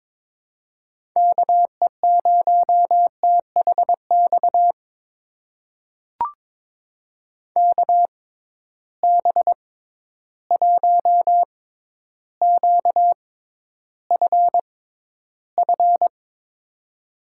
Morse code, heard as a keyed tone, about 23 wpm, spaced as for 11 wpm, 700 Hz. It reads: KE0THX KB1QFF
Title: USA Call Signs - 6 Characters - 6x Character Spacing - 22wpm